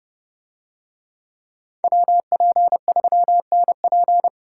Morse code, heard as a keyed tone, about 30 wpm, 700 Hz. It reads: WP3NP